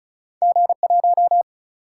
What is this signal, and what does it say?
Morse code, heard as a keyed tone, about 36 wpm, 700 Hz. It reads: G1
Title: Single-Letter Single-Number 35wpm